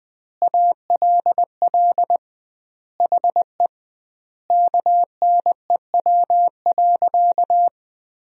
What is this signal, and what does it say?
Morse code, heard as a keyed tone, about 20 wpm, 700 Hz. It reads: ALL HE KNEW.
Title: 2x Word-Spacing - Sentences from Top 400 Words - 20wpm